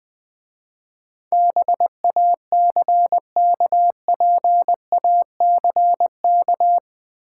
Morse code, heard as a keyed tone, about 20 wpm, 700 Hz. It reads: BACKPACK
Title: Difficult Words - Speed-Racing with Hints - 20wpm